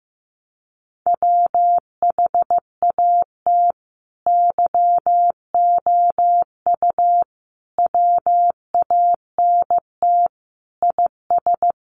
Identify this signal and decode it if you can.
Morse code, heard as a keyed tone, about 14 wpm, 700 Hz. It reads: WHAT YOU WANT IS
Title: Sentences from Top 700 Words 15wpm